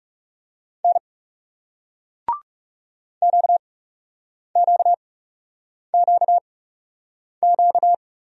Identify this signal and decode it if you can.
Morse code, heard as a keyed tone, about 38 wpm, spaced as for 8 wpm, 700 Hz. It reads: N QQQQ